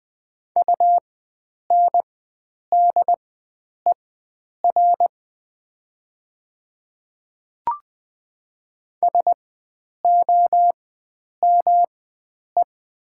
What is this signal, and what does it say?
Morse code, heard as a keyed tone, about 20 wpm, spaced as for 9 wpm, 700 Hz. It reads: UNDER SOME
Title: 4x Character Spacing - Top 300 Words - 20wpm